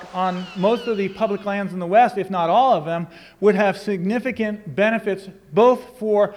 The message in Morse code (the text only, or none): none